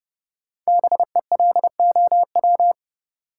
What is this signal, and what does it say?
Morse code, heard as a keyed tone, about 30 wpm, 700 Hz. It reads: BELOW
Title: Sentences from Top 1000 Words - Review Part 5 - New Word Drill - 30wpm